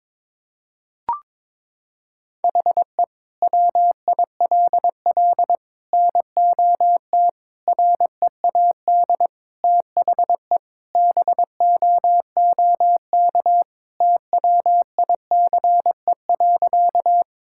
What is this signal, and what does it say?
Morse code, heard as a keyed tone, about 22 wpm, 700 Hz. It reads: HE WILL NOT READ THE BOOK TWICE.
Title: Sentences from Top 1600 Words 22wpm